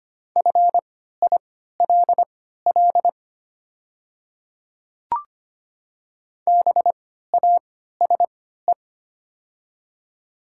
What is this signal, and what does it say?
Morse code, heard as a keyed tone, about 25 wpm, spaced as for 14 wpm, 700 Hz. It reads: FILL BASE
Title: 3x Character Spacing - Top 500 Words - 25wpm